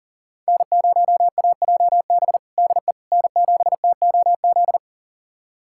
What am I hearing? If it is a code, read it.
Morse code, heard as a keyed tone, about 37 wpm, 700 Hz. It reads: N0AJB DE N7TOZ
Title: USA Callsign DE Callsign - 5 Characters - 40wpm